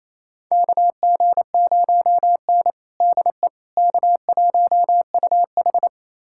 Morse code, heard as a keyed tone, about 27 wpm, 700 Hz. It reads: KG0N DE K1UH
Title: USA Callsign DE Callsign - 4 Characters - 28wpm